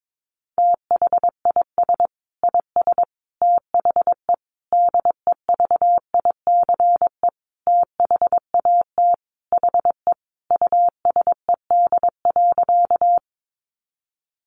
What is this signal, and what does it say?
Morse code, heard as a keyed tone, about 22 wpm, 700 Hz. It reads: THIS IS THE DEVICE THAT HE USED.
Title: Sentences from Top 1500 Words 22wpm